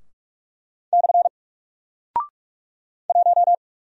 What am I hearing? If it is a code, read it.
Morse code, heard as a keyed tone, about 44 wpm, 700 Hz. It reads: / 1
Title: Single Character (Letter:Number:Punctuation:Prosign) 45wpm